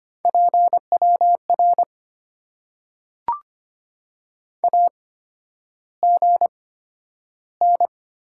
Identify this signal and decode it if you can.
Morse code, heard as a keyed tone, about 25 wpm, spaced as for 11 wpm, 700 Hz. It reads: PWR AGN
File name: Top 90 QSO Elements - 8x Character Spacing - 25wpm